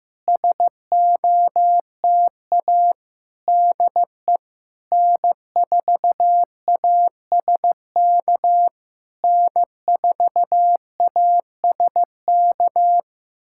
Morse code, heard as a keyed tone, about 15 wpm, 700 Hz. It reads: SOTA DE N4ASK N4ASK
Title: SOTA QSO Practice 15wpm